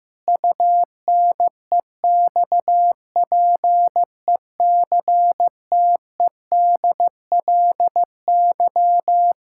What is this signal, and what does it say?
Morse code, heard as a keyed tone, about 16 wpm, 700 Hz. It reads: UNEXPECTEDLY
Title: Pangram Challenge v2 - 15wpm